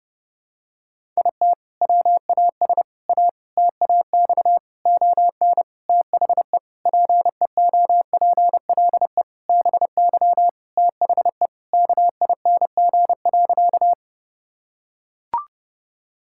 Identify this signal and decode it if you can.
Morse code, heard as a keyed tone, about 30 wpm, 700 Hz. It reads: IT WAS A TAX ON THE PEOPLE BY THE KING.
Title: Sentences from Top 1000 Words - Review Part 5 - 30wpm